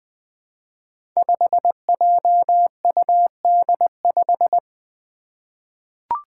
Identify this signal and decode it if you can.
Morse code, heard as a keyed tone, about 20 wpm, 700 Hz. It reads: 5JUD5
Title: Five Challenging Letter-Number Combinations 20wpm